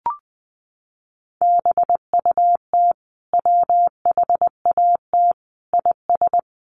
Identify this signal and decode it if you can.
Morse code, heard as a keyed tone, about 20 wpm, 700 Hz. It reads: BUT WHAT IS
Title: Sentences from Top 800 Words 20wpm